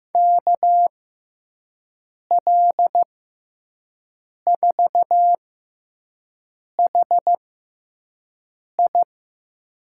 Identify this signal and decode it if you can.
Morse code, heard as a keyed tone, about 15 wpm, spaced as for 5 wpm, 700 Hz. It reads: KL4HI
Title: USA Call Signs - 5 Characters - 6x Character Spacing - 15wpm